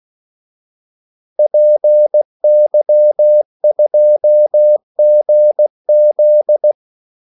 Morse code, heard as a keyed tone, about 16 wpm, 600 Hz. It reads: PY2GZ
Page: Kaldesignaler 16 wpm